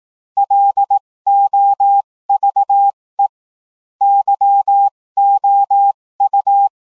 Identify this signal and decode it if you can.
Morse code, heard as a keyed tone, about 18 wpm, spaced as for 15 wpm, 800 Hz. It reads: LOVE YOU